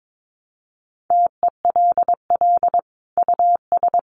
Morse code, heard as a keyed tone, about 22 wpm, 700 Hz. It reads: TELL US
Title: Sentences from Top 200 Words 22wpm